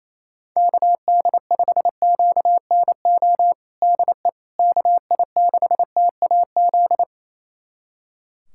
Morse code, heard as a keyed tone, about 28 wpm, 700 Hz. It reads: KD5QNO DE KI6TAZ
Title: USA Callsign DE Callsign 28wpm